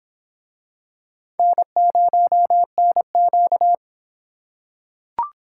Morse code, heard as a keyed tone, about 25 wpm, 700 Hz. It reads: N0NQ